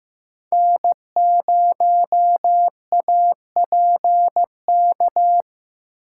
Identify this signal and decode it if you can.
Morse code, heard as a keyed tone, about 16 wpm, 700 Hz. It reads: N0APK
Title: USA Call Signs - 4x Character Spacing - 15wpm